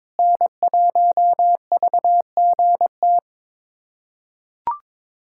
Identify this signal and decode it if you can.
Morse code, heard as a keyed tone, about 22 wpm, 700 Hz. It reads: N1VGT
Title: USA Callsign DE Callsign - 5 Characters - 22wpm